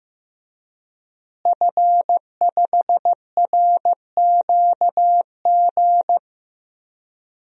Morse code, heard as a keyed tone, about 15 wpm, 700 Hz. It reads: F5RQG
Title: DXCC-100 Call Signs - 5 Characters - Repeated 3x - 15wpm